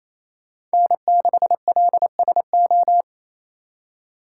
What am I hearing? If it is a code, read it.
Morse code, heard as a keyed tone, about 36 wpm, 700 Hz. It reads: N6LSO